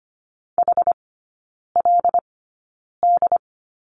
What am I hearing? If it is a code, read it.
Morse code, heard as a keyed tone, about 25 wpm, spaced as for 9 wpm, 700 Hz. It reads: HLD
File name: course / 25:10wpm / 093 Lesson 16e - Letter c - Three Characters 25wpm